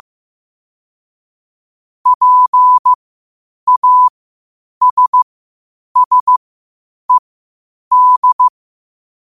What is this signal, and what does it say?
Morse code, heard as a keyed tone, about 15 wpm, spaced as for 8 wpm, 1000 Hz. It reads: PASSED
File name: qotdfiles / 8wpm